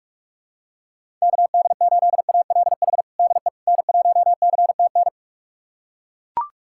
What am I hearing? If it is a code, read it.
Morse code, heard as a keyed tone, about 42 wpm, 700 Hz. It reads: KD8ALH DE N1CTN
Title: USA Callsign DE Callsign 45wpm